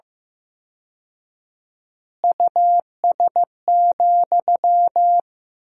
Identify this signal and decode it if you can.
Morse code, heard as a keyed tone, about 15 wpm, 700 Hz. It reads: US,
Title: 4x Word-Spacing - Sentences from Top 400 Words - 15wpm